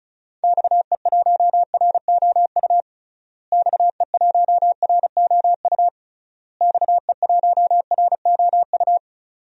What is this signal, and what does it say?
Morse code, heard as a keyed tone, about 34 wpm, 700 Hz. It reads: XE1ROU XE1ROU XE1ROU
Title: DXCC-100 Call Signs - 6 Characters - Repeated 3x - 35wpm